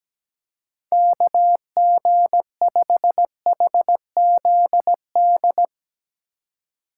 Morse code, heard as a keyed tone, about 17 wpm, 700 Hz. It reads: KG5HZD
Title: USA Call Signs - 6 Characters - 4x Character Spacing - 17wpm